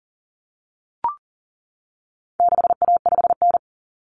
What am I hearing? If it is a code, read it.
Morse code, heard as a keyed tone, about 40 wpm, 700 Hz. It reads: 6A5N
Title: Common Cut Numbers - 4-Digit 40wpm